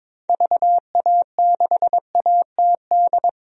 Morse code, heard as a keyed tone, about 22 wpm, 700 Hz. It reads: VA6ATD
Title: DXCC-100 Call Signs - 6 Characters - 22wpm